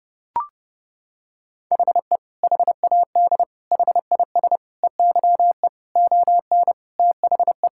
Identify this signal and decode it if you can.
Morse code, heard as a keyed tone, about 32 wpm, 700 Hz. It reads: HE HAD HIS EYE ON THE
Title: Sentences from Top 1000 Words - Review Part 5 - 30wpm